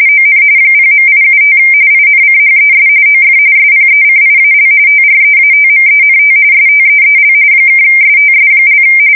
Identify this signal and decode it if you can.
RTTY, about 45.45 baud. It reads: ON 0.875 INCHES OF SPECIAL TREATMENT STEEL (STS)
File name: fsk170 / USS-New-Jersey